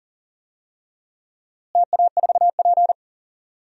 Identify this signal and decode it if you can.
Morse code, heard as a keyed tone, about 40 wpm, 700 Hz. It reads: TA4P
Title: DXCC-100 Call Signs - 4 Characters - Repeated 2x - 40wpm